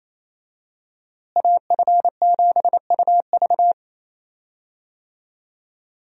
Morse code, heard as a keyed tone, about 28 wpm, 700 Hz. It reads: AF7UV
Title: USA Call Signs - 5 Characters - Repeated 2x - 28wpm